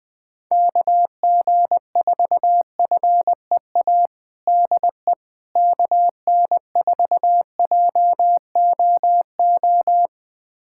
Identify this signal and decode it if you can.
Morse code, heard as a keyed tone, about 20 wpm, 700 Hz. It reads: KG4FEA DE KN4JOO
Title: USA Callsign DE Callsign - 6 Characters - 20wpm